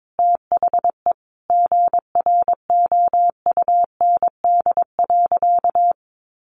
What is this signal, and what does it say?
Morse code, heard as a keyed tone, about 22 wpm, 700 Hz. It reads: THE GROUND.
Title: Sentences from Top 500 Words 22wpm